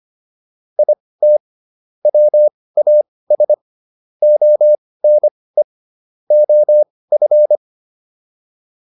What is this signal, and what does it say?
Morse code, heard as a keyed tone, about 25 wpm, spaced as for 18 wpm, 600 Hz. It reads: IT WAS ONE OF